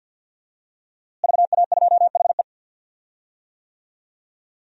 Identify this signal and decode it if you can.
Morse code, heard as a keyed tone, about 45 wpm, 700 Hz. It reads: VA2HE
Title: DXCC-100 Call Signs - 5 Characters - Repeated 2x - 50wpm